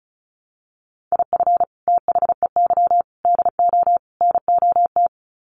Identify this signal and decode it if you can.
Morse code, heard as a keyed tone, about 34 wpm, 700 Hz. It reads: IF THEY DO NOT
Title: Sets of 4 Words 35wpm